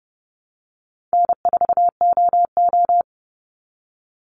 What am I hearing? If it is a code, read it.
Morse code, heard as a keyed tone, about 30 wpm, 700 Hz. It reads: N4OO
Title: Call Signs Repeated 2x - 30wpm